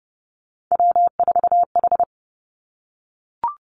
Morse code, heard as a keyed tone, about 30 wpm, 700 Hz. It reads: W4H
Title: Three Challenging Letter-Number Combinations 30wpm